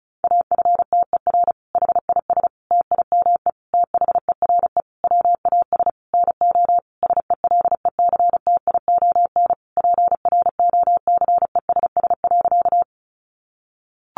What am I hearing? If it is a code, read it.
Morse code, heard as a keyed tone, about 34 wpm, 700 Hz. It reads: AFTER HIS TIME THERE WAS NO SELECTION PROCESS.
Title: Sentences from Top 1400 Words 35wpm